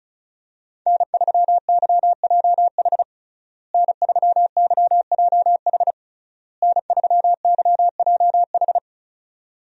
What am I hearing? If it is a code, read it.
Morse code, heard as a keyed tone, about 36 wpm, 700 Hz. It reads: N3YJH N3YJH N3YJH
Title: USA Call Signs - 5 Characters - Repeated 3x - 35wpm